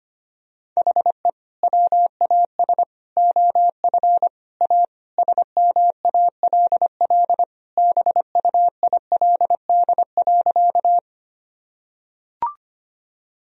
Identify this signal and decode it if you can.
Morse code, heard as a keyed tone, about 25 wpm, 700 Hz. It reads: HE WAS OF A SMALL BUILD.
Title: Sentences from Top 1000 Words - Review Part 1 - 25wpm